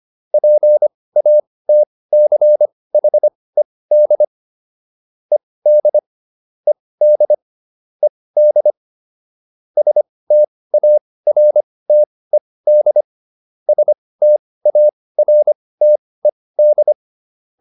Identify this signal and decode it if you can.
Morse code, heard as a keyed tone, about 25 wpm, spaced as for 18 wpm, 600 Hz. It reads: PATCHED ED ED ED STARTED STARTED
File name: ED7 18